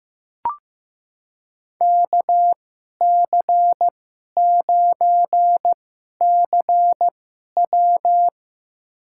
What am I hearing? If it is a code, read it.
Morse code, heard as a keyed tone, about 15 wpm, spaced as for 11 wpm, 700 Hz. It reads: KC9CW
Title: USA Call Signs - 5 Characters - 2x Character Spacing - 15wpm